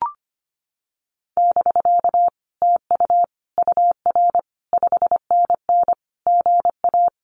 Morse code, heard as a keyed tone, about 25 wpm, 700 Hz. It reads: <BK> TU UR 5NN GA
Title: Common QSO Phrases Encore 25wpm